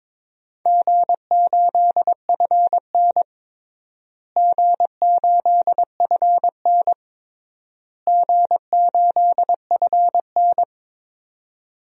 Morse code, heard as a keyed tone, about 22 wpm, 700 Hz. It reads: G8FN G8FN G8FN
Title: DXCC-100 Call Signs - 4 Characters - Repeated 3x - 22wpm